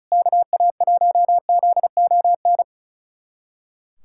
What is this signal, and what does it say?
Morse code, heard as a keyed tone, about 36 wpm, 700 Hz. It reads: KA1ZON
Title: USA Callsign DE Callsign - 6 Characters - 35wpm